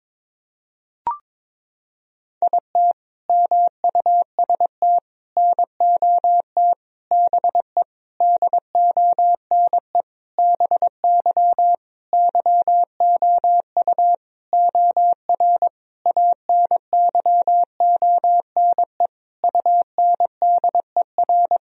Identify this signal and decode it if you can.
Morse code, heard as a keyed tone, about 22 wpm, 700 Hz. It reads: IT MUST NOT BE DONE BY YOU OR ANYONE UNDER
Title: Sentences from Top 1000 Words - Review Part 1 - 22wpm